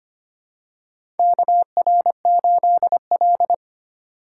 Morse code, heard as a keyed tone, about 25 wpm, 700 Hz. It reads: KR8L
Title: K1USN SST QSO Practice - 25wpm